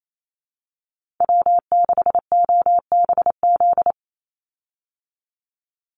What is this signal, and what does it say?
Morse code, heard as a keyed tone, about 28 wpm, 700 Hz. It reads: W6OBZ
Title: USA Call Signs 28wpm